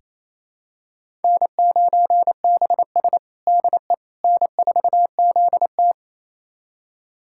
Morse code, heard as a keyed tone, about 28 wpm, 700 Hz. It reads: N9BS DE N4ZT
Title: USA Callsign DE Callsign - 4 Characters - 28wpm